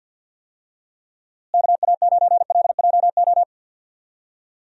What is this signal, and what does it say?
Morse code, heard as a keyed tone, about 47 wpm, 700 Hz. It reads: KA9LJO
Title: USA Call Signs - 6 Characters - Repeated 2x - 50wpm